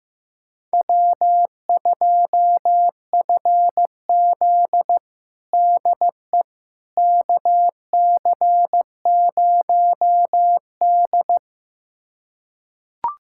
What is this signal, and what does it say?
Morse code, heard as a keyed tone, about 15 wpm, 700 Hz. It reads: W2FZ DE KC0D
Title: USA Callsign DE Callsign 15wpm